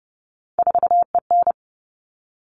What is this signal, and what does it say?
Morse code, heard as a keyed tone, about 30 wpm, 700 Hz. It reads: 4EN
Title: All Cut Numbers - 3-Digit 30wpm